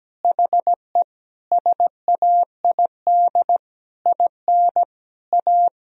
Morse code, heard as a keyed tone, about 17 wpm, 700 Hz. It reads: HE SAID IN A
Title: Sentences from Top 1000 Words - Review Part 1 - 17wpm